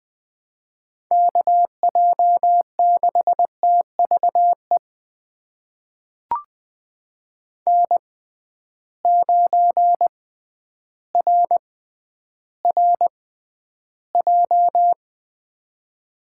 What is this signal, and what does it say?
Morse code, heard as a keyed tone, about 20 wpm, spaced as for 11 wpm, 700 Hz. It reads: KJ6TVE N9RRJ
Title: USA Call Signs - 6x Character Spacing - 20wpm